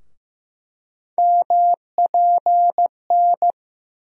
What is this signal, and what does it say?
Morse code, heard as a keyed tone, about 15 wpm, 700 Hz. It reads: MPN
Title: USA Call Sign Postfixes - 3 Letters 15wpm